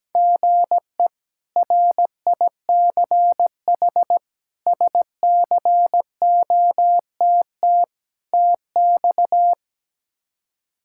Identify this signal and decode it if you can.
Morse code, heard as a keyed tone, about 17 wpm, 700 Hz. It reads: GE RICH SCOTT TX
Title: K1USN SST QSO Practice - 17wpm